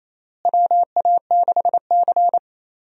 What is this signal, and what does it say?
Morse code, heard as a keyed tone, about 28 wpm, 700 Hz. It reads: WA6C
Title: USA Call Signs - 4 Characters - Repeated 3x - 28wpm